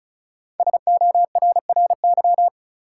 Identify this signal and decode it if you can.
Morse code, heard as a keyed tone, about 36 wpm, 700 Hz. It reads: SORRY
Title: Five Letter Words - 35wpm